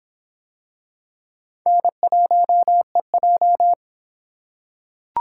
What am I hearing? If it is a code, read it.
Morse code, heard as a keyed tone, about 23 wpm, 700 Hz. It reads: N1EJ